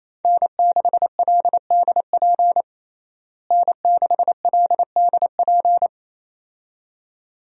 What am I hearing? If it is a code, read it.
Morse code, heard as a keyed tone, about 28 wpm, 700 Hz. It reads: N6LDP N6LDP
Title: USA Call Signs - 5 Characters - Repeated 2x - 28wpm